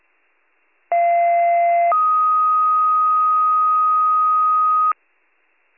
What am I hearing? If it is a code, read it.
Two-tone page, heard as a one-second tone, then a three-second tone, about 690 then 1250 Hz, with noise. It is heard from 0.9 to 4.9 s.